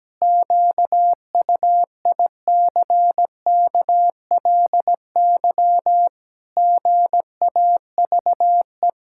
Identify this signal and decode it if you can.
Morse code, heard as a keyed tone, about 17 wpm, 700 Hz. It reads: QUICKLY GAVE
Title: Pangram Challenge v2 - 17wpm